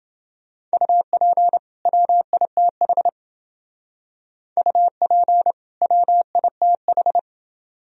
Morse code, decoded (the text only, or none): UP WITH UP WITH